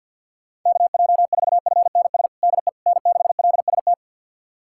Morse code, heard as a keyed tone, about 46 wpm, 700 Hz. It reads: KJ4FNS DE N6LST
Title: USA Callsign DE Callsign 50wpm